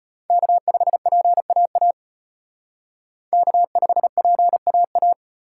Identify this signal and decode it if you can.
Morse code, heard as a keyed tone, about 32 wpm, 700 Hz. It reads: K5PAA K5PAA